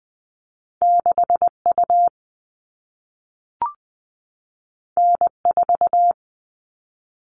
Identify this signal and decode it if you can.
Morse code, heard as a keyed tone, about 20 wpm, 700 Hz. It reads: 6U N4
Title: All Cut Numbers - 2-Digit 20wpm